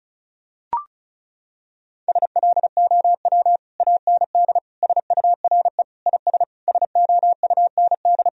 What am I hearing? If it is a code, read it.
Morse code, heard as a keyed tone, about 34 wpm, 700 Hz. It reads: SLOW AND SURE IS SOUND
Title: Sentences from Top 1000 Words - Review Part 2 - 35wpm